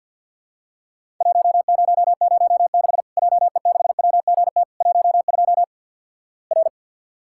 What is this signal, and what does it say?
Morse code, heard as a keyed tone, about 50 wpm, 700 Hz. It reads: 1006 JE6WGT 12 R
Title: Worked All Europe DX Contest - QTC Exchange with Repeats - 50wpm